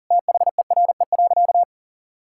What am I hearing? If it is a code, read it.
Morse code, heard as a keyed tone, about 40 wpm, 700 Hz. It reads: THERE.
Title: Sentences from Top 1000 Words - Review Part 5 - 40wpm